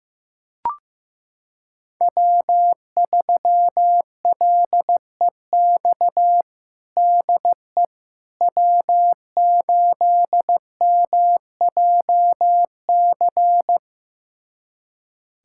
Morse code, heard as a keyed tone, about 16 wpm, 700 Hz. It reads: W3LEX DE W8MJC
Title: USA Callsign DE Callsign - 5 Characters - 15wpm